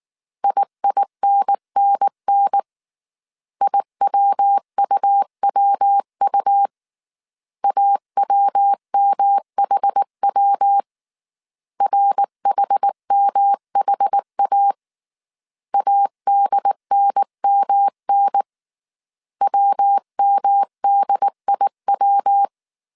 Morse code, heard as a keyed tone, about 19 wpm, 800 Hz. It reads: IINNN IWUWU AWMHW RHMHA ADNMN WMDIW
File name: CW / DET / 1 3 1500C S B 18wpm 1916